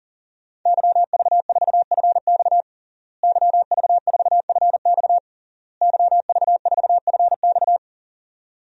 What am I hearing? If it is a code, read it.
Morse code, heard as a keyed tone, about 40 wpm, 700 Hz. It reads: YV4FX YV4FX YV4FX
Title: DXCC-100 Call Signs - 5 Characters - Repeated 3x - 40wpm